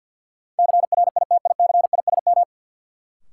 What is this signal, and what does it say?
Morse code, heard as a keyed tone, about 53 wpm, 700 Hz. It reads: CRITICISM
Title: Sentences from Top 1300 Words - New Word Drill - 50wpm